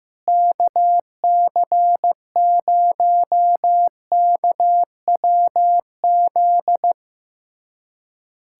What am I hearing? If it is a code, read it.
Morse code, heard as a keyed tone, about 15 wpm, 700 Hz. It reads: KC0KWZ
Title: USA Callsign DE Callsign - 6 Characters - 15wpm